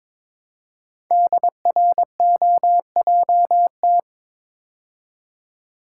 Morse code, heard as a keyed tone, about 22 wpm, 700 Hz. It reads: DROJT
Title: Code Groups - 5 Letters - 22wpm